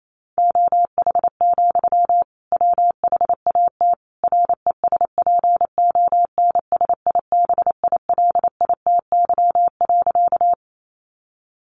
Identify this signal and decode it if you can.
Morse code, heard as a keyed tone, about 28 wpm, 700 Hz. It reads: OH, WHAT RESPONSIBILITY.
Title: Sentences from Top 1200 Words 28wpm